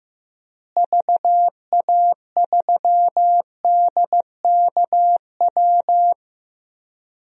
Morse code, heard as a keyed tone, about 15 wpm, 700 Hz. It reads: VA3DKW
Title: DXCC-100 Call Signs - 6 Characters - Repeated 3x - 15wpm